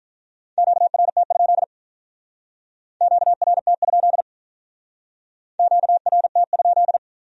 Morse code, heard as a keyed tone, about 43 wpm, 700 Hz. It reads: QRT? QRT? QRT?
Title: Speed-Racing Q-Codes - 35wpm